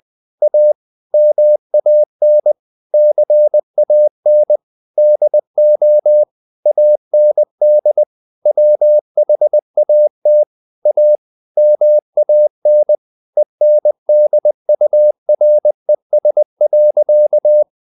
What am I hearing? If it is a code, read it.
Morse code, heard as a keyed tone, about 20 wpm, 600 Hz. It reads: A MAN CAN DO AND WHAT A MAN ENDURES.
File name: the old man and the sea / the old man and the sea part 1 20wpm